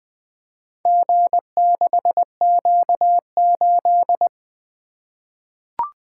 Morse code, heard as a keyed tone, about 20 wpm, 700 Hz. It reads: G6Q8
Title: Four Letter-Number Combinations - 20wpm